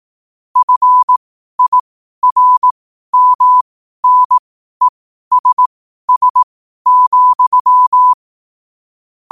Morse code, heard as a keyed tone, about 18 wpm, spaced as for 12 wpm, 1000 Hz. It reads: FIRMNESS,
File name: qotdfiles / 12wpm